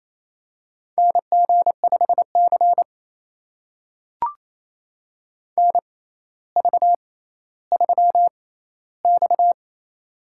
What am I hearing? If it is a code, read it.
Morse code, heard as a keyed tone, about 28 wpm, spaced as for 14 wpm, 700 Hz. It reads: NG5C NV3X
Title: USA Call Signs - 4 Characters - 6x Character Spacing - 28wpm